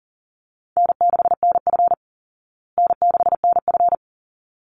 Morse code, heard as a keyed tone, about 40 wpm, 700 Hz. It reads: N6NF N6NF
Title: Easy Call Signs Repeated 2x - 40wpm